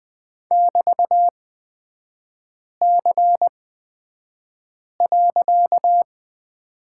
Morse code, heard as a keyed tone, about 20 wpm, spaced as for 5 wpm, 690 Hz. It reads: <BT>C.
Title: 255 Lesson 41e - Period - Three Characters 20wpm